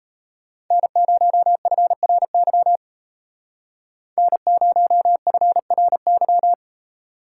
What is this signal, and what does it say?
Morse code, heard as a keyed tone, about 32 wpm, 700 Hz. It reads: N0FRY N0FRY
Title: Speed Racing - USA 1x3 Call Signs 28wpm